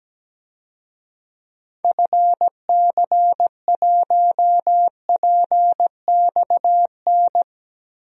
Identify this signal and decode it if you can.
Morse code, heard as a keyed tone, about 17 wpm, 700 Hz. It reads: FC1PXN
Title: DXCC-100 Call Signs - 6 Characters - 17wpm